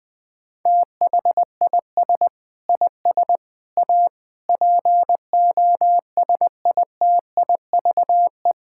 Morse code, heard as a keyed tone, about 20 wpm, 700 Hz. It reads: THIS IS A POSITIVE
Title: Sentences from Top 1000 Words - Review Part 5 - 20wpm